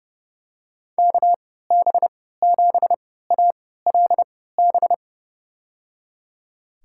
Morse code, heard as a keyed tone, about 30 wpm, spaced as for 17 wpm, 700 Hz. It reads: KB7ALB
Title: USA Call Signs - 3x Character Spacing - 30wpm